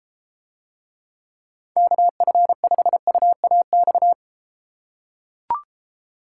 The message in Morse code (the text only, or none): KF5UAX